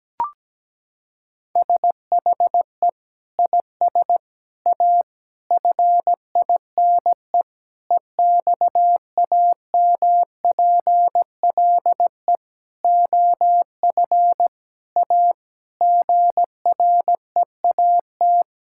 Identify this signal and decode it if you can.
Morse code, heard as a keyed tone, about 17 wpm, 700 Hz. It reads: SHE IS A FINE EXAMPLE OF A GREAT
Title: Sentences from Top 1000 Words - Review Part 6 - 17wpm